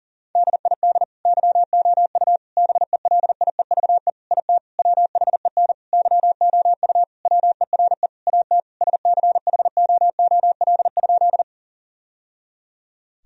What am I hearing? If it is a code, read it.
Morse code, heard as a keyed tone, about 40 wpm, 700 Hz. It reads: DID YOU BELIEVE IT WHEN YOU WERE AT SCHOOL?
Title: Sentences from Top 1000 Words - Review Part 1 - 40wpm